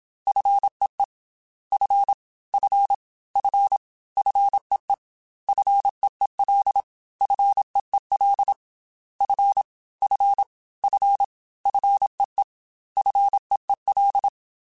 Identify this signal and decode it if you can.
Morse code, heard as a keyed tone, about 27 wpm, 800 Hz. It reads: FEE F F F FEE FEEL FEEL F F F FEE FEEL
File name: F and L / F and L mp3 / F and L Prefixes 3x2 Twice @25